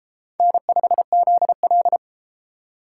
Morse code, heard as a keyed tone, about 32 wpm, 700 Hz. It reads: N5ZL